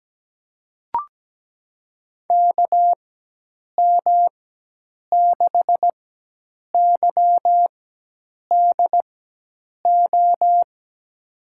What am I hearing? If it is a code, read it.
Morse code, heard as a keyed tone, about 17 wpm, spaced as for 8 wpm, 700 Hz. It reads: KM6YDO
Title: USA Call Signs - 4x Character Spacing - 17wpm